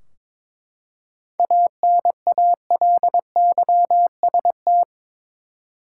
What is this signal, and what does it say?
Morse code, heard as a keyed tone, about 22 wpm, 700 Hz. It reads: ANALYST
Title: Practice QSO - Exchange 3 - More Info - 22wpm